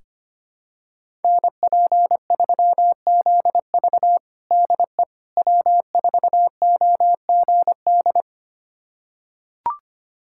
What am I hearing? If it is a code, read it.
Morse code, heard as a keyed tone, about 25 wpm, 700 Hz. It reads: NP3ZV DE W4OGD
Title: USA Callsign DE Callsign - 5 Characters - 25wpm